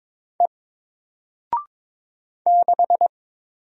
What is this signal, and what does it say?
Morse code, heard as a keyed tone, about 22 wpm, 700 Hz. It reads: E 6
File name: Single-Letter-Number-22wpm